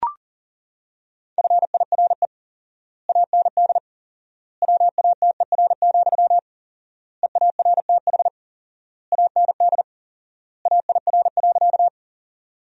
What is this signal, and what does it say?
Morse code, heard as a keyed tone, about 40 wpm, 700 Hz. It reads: FIRE AND WATER, EARTH AND AIR.
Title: 4x Word-Spacing - Sentences from Top 400 Words - 40wpm